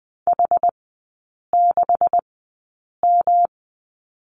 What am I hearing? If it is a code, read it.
Morse code, heard as a keyed tone, about 20 wpm, spaced as for 8 wpm, 700 Hz. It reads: H6M
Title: 138 Lesson 23e - Letter f - Three Characters 20wpm